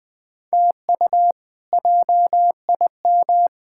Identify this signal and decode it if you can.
Morse code, heard as a keyed tone, about 20 wpm, 700 Hz. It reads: TU JIM
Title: K1USN SST QSO Practice - 20wpm